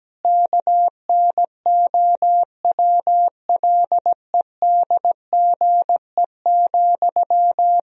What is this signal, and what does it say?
Morse code, heard as a keyed tone, about 17 wpm, 700 Hz. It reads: KNOWLEDGE,
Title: Sentences from Top 1000 Words - Review Part 3 - 17wpm